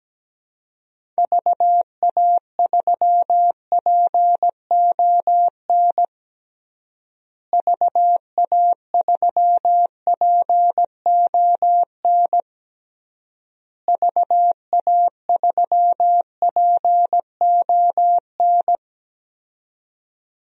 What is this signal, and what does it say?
Morse code, heard as a keyed tone, about 17 wpm, 700 Hz. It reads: VA3PON VA3PON VA3PON
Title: DXCC-100 Call Signs - 6 Characters - Repeated 3x - 17wpm